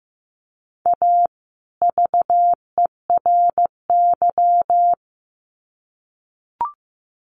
Morse code, heard as a keyed tone, about 15 wpm, 700 Hz. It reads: A VERY
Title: Sets of 2 Words 15wpm